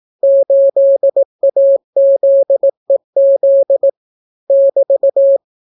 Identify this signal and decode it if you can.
Morse code, heard as a keyed tone, about 18 wpm, 550 Hz. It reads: 8AZEZ <BT>